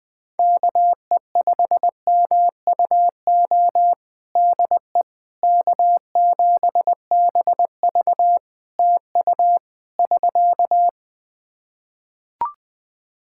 Morse code, heard as a keyed tone, about 21 wpm, 700 Hz. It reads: KE5MUO DE K7BV TU <SK>
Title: K1USN SST QSO Practice 20wpm